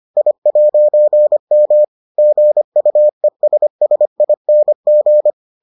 Morse code, heard as a keyed tone, about 25 wpm, 600 Hz. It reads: I'M GUESSING